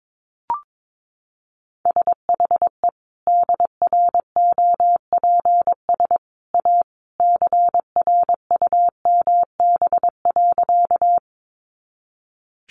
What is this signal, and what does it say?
Morse code, heard as a keyed tone, about 22 wpm, 700 Hz. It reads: SHE DROPS A CRUMB.
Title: Additional Easy Phrases 22wpm